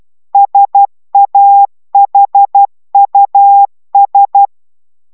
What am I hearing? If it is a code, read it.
Morse code, heard as a keyed tone, about 12 wpm, 800 Hz. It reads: SAHUS
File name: taviro tanulas / cw 69 04 10